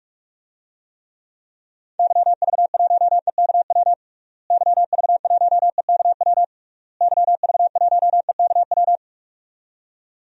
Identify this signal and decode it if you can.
Morse code, heard as a keyed tone, about 41 wpm, 700 Hz. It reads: YV1EKW YV1EKW YV1EKW
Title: DXCC-100 Call Signs - 6 Characters - Repeated 3x - 45wpm